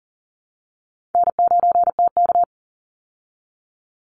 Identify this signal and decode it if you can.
Morse code, heard as a keyed tone, about 40 wpm, 700 Hz. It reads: N9TK
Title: Easy Call Signs Repeated 3x - 40wpm